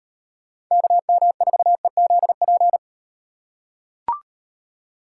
Morse code, heard as a keyed tone, about 38 wpm, 700 Hz. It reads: KM4EZP